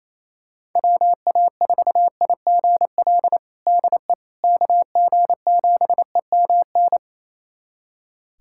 Morse code, heard as a keyed tone, about 28 wpm, 700 Hz. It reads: WA4IGL DE KG7EMN
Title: USA Callsign DE Callsign - 6 Characters - 28wpm